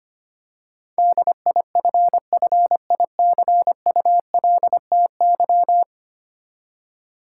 Morse code, heard as a keyed tone, about 25 wpm, 700 Hz. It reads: DIFFICULTY
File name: Ten Letter Words - 25wpm